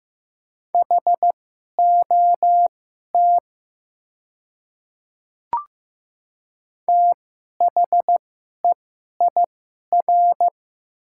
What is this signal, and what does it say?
Morse code, heard as a keyed tone, about 15 wpm, spaced as for 11 wpm, 700 Hz. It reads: HOT THEIR
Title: 2x Character Spacing - Top 300 Words - 15wpm